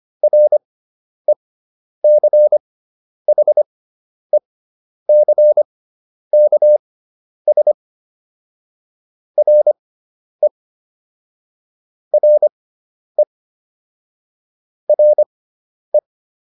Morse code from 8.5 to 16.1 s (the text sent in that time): RE RE RE